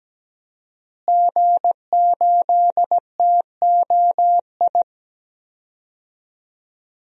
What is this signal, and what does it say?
Morse code, heard as a keyed tone, about 17 wpm, 700 Hz. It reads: G8TOI
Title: DXCC-100 Call Signs - 5 Characters - Repeated 3x - 17wpm